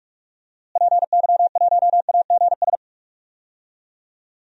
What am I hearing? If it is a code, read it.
Morse code, heard as a keyed tone, about 48 wpm, 700 Hz. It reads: PY1AGS
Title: DXCC-100 Call Signs - 6 Characters - Repeated 2x - 45wpm